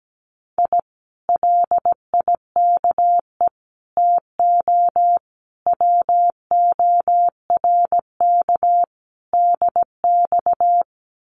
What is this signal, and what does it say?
Morse code, heard as a keyed tone, about 17 wpm, 700 Hz. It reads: I LIKE TO WORK DX
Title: Common QSO Phrases Encore 17wpm